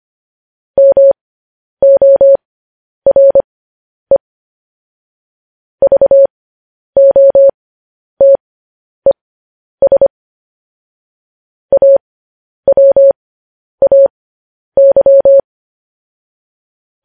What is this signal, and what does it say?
Morse code, heard as a keyed tone, about 25 wpm, spaced as for 10 wpm, 550 Hz. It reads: MORE VOTES AWAY